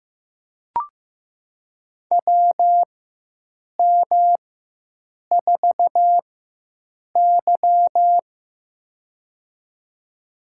Morse code, heard as a keyed tone, about 15 wpm, spaced as for 7 wpm, 700 Hz. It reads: WM4Y
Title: USA Call Signs - 4 Characters - 4x Character Spacing - 15wpm